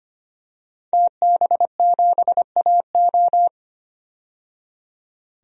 Morse code, heard as a keyed tone, about 25 wpm, 700 Hz. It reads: TB7AO
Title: DXCC-100 Call Signs - 5 Characters - 25wpm